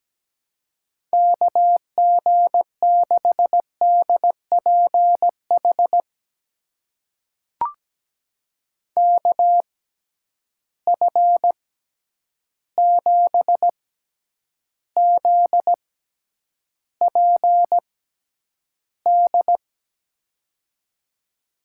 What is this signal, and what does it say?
Morse code, heard as a keyed tone, about 17 wpm, spaced as for 9 wpm, 700 Hz. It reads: KG6DPH KF7ZPD
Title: USA Call Signs - 6 Characters - 6x Character Spacing - 17wpm